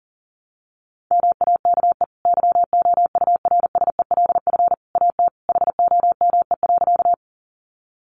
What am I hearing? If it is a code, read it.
Morse code, heard as a keyed tone, about 40 wpm, 700 Hz. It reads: MAKE YOURSELF AT HOME.
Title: Sentences from Top 600 Words 40wpm